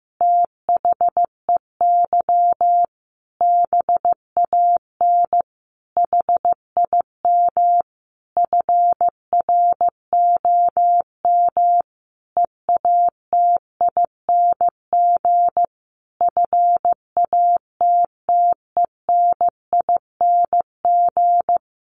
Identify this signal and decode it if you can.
Morse code, heard as a keyed tone, about 15 wpm, 700 Hz. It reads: THEY BAN HIM FROM EATING FATTENING
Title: Easy Phrases 15wpm